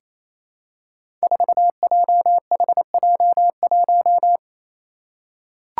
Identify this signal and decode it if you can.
Morse code, heard as a keyed tone, about 28 wpm, 700 Hz. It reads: 4JHJ1
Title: Five Challenging Letter-Number Combinations 28wpm